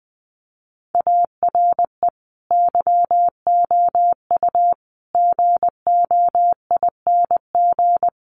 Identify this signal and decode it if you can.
Morse code, heard as a keyed tone, about 20 wpm, 700 Hz. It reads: ARE YOU GOING
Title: Sets of 5 Words 20wpm